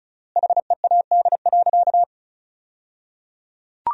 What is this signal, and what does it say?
Morse code, heard as a keyed tone, about 36 wpm, 700 Hz. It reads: HEAD.
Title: 3x Word-Spacing - Sentences from Top 300 Words - 35wpm